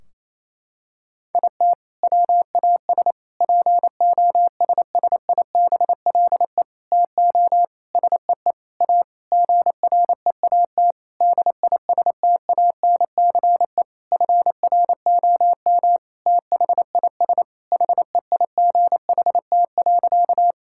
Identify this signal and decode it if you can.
Morse code, heard as a keyed tone, about 27 wpm, 700 Hz. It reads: IT WAS POSSIBLE TO SEE A GREAT DISTANCE FROM THIS HEIGHT.
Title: Sentences from Top 1600 Words 28wpm